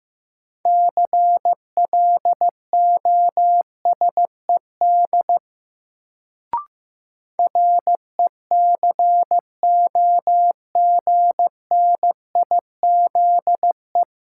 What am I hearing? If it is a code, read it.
Morse code, heard as a keyed tone, about 15 wpm, 700 Hz. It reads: CLOSED RECOGNIZE
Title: Sentences from Top 1400 Words - New Word Drill - 15wpm